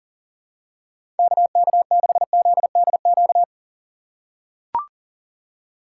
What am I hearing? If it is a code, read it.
Morse code, heard as a keyed tone, about 40 wpm, 700 Hz. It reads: KK6ZDQ
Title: USA Call Signs - 6 Characters - 6x Character Spacing - 40wpm